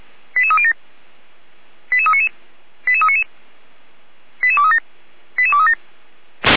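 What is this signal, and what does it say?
Five-tone selective call (ZVEI), heard as quick runs of five tones, tones of about 70 ms. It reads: 88307 88399 88399 88237 88237
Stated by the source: ZVEI five-tone selective call received on an SDR by the Feuerwehr Orlamünde fire brigade.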